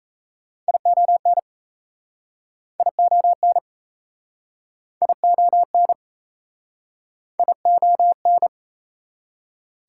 Morse code, heard as a keyed tone, about 40 wpm, 700 Hz. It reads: ION ION ION ION